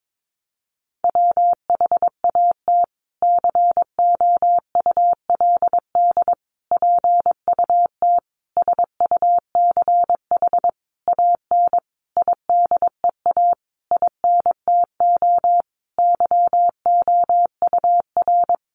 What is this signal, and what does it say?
Morse code, heard as a keyed tone, about 22 wpm, 700 Hz. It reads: WHAT COULD PUT SUCH AN IDEA INTO YOUR
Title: Sentences from Top 600 Words 22wpm